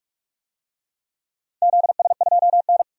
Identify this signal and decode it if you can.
Morse code, heard as a keyed tone, about 44 wpm, 690 Hz. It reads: ZS2N
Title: DXCC-100 Call Signs - 4 Characters - 45wpm